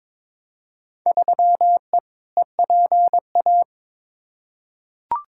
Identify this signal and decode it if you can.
Morse code, heard as a keyed tone, about 22 wpm, 700 Hz. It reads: 3E EPA
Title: Field Day Exchanges Encore - 22wpm